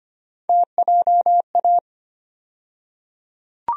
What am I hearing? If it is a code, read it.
Morse code, heard as a keyed tone, about 25 wpm, 700 Hz. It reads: TJA
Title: USA Call Sign Postfixes - 3 Letters 25wpm